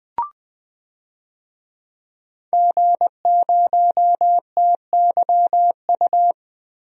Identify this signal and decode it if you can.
Morse code, heard as a keyed tone, about 21 wpm, 700 Hz. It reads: G0TYU
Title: DXCC-100 Call Signs - 5 Characters - Repeated 2x - 20wpm